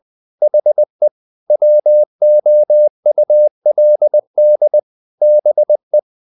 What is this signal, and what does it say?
Morse code, heard as a keyed tone, about 20 wpm, 600 Hz. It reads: HE WOULD BE
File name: the old man and the sea / the old man and the sea part 1 20wpm